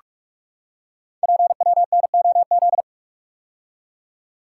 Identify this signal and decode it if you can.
Morse code, heard as a keyed tone, about 44 wpm, 700 Hz. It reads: PWNOZ
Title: Code Groups - 5 Letters - 45wpm